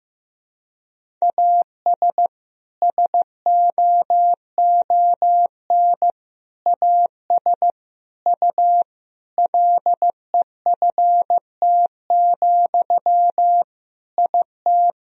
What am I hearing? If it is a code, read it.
Morse code, heard as a keyed tone, about 14 wpm, 700 Hz. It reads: AS SOON AS U LEFT, IT
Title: Sentences from Top 400 Words and Abbreviations 15wpm